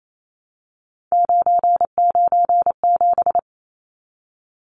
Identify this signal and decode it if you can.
Morse code, heard as a keyed tone, about 28 wpm, 700 Hz. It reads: 997
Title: Numbers - 3-Digit 28wpm